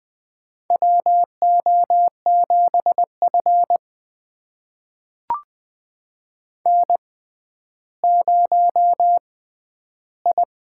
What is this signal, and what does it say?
Morse code, heard as a keyed tone, about 20 wpm, spaced as for 11 wpm, 700 Hz. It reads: WO7F N0I
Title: USA Call Signs - 6x Character Spacing - 20wpm